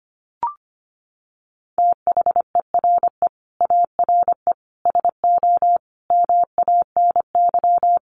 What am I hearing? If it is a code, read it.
Morse code, heard as a keyed tone, about 25 wpm, 700 Hz. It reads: THERE ARE SO MANY
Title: Sets of 4 Words 25wpm